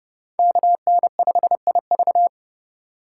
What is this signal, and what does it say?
Morse code, heard as a keyed tone, about 30 wpm, 700 Hz. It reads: KN5IV
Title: Code Groups - 5 Letter:Numbers - 30wpm